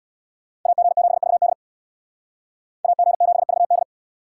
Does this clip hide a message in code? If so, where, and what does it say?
Yes: Morse code from 0.7 to 3.8 s, heard as a keyed tone, about 65 wpm, 700 Hz. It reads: AG7VD AG7VD